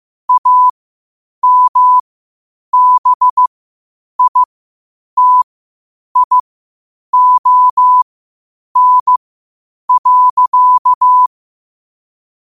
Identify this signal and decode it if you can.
Morse code, heard as a keyed tone, about 15 wpm, spaced as for 8 wpm, 1000 Hz. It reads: AMBITION.
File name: qotdfiles / 8wpm